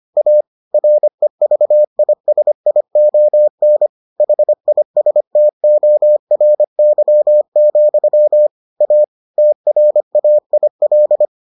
Morse code, heard as a keyed tone, about 25 wpm, 600 Hz. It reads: A REVISION HISTORY, A TRAIL